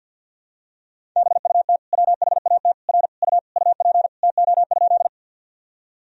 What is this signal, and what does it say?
Morse code, heard as a keyed tone, about 53 wpm, 700 Hz. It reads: BUT WHAT R U UP TO?
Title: Sentences from Top 100 Words and Abbreviations 50wpm